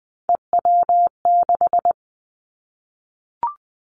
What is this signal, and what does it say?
Morse code, heard as a keyed tone, about 20 wpm, 700 Hz. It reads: EW6
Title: Callsign Prefixes for Top 50 DXCC - 20wpm